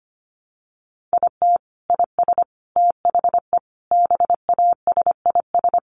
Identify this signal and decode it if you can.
Morse code, heard as a keyed tone, about 25 wpm, 700 Hz. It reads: IT IS THE BASIS
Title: Sentences from Top 1200 Words 25wpm